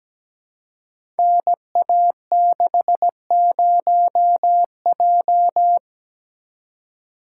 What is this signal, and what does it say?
Morse code, heard as a keyed tone, about 17 wpm, 700 Hz. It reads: NA60J
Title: Code Groups - 5 Letter:Numbers - 17wpm